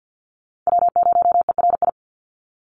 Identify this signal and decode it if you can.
Morse code, heard as a keyed tone, about 50 wpm, 700 Hz. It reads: W0ERI
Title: NA SOTA Chaser Callsigns 50wpm